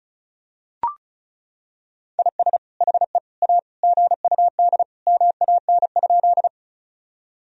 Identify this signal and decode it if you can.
Morse code, heard as a keyed tone, about 36 wpm, 700 Hz. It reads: IS HE A GUD MAN?
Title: Sentences from Top 200 Words and Abbreviations 35wpm